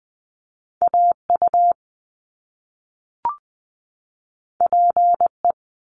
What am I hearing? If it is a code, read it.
Morse code, heard as a keyed tone, about 20 wpm, 700 Hz. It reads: A U P E
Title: Top Two Letter Combinations in Words 20wpm